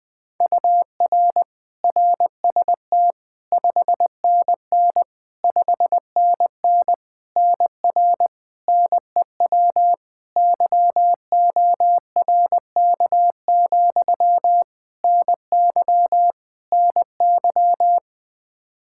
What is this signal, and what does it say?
Morse code, heard as a keyed tone, about 20 wpm, 700 Hz. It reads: UR RST 5NN 5NN NR NEW YORK, NY NY
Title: K1USN SST QSO Practice 20wpm